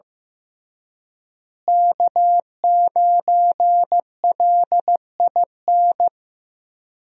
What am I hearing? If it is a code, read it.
Morse code, heard as a keyed tone, about 15 wpm, 700 Hz. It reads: K9LIN
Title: USA Call Signs - 5 Characters - Repeated 3x - 15wpm